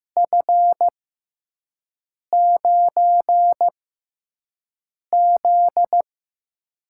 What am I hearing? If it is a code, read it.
Morse code, heard as a keyed tone, about 15 wpm, spaced as for 5 wpm, 700 Hz. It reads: F9Z
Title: USA Call Signs - 4 Characters - 6x Character Spacing - 15wpm